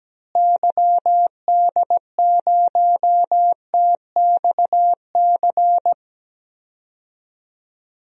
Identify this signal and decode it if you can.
Morse code, heard as a keyed tone, about 17 wpm, 700 Hz. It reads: YD0TXC
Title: DXCC-100 Call Signs - 6 Characters - Repeated 3x - 17wpm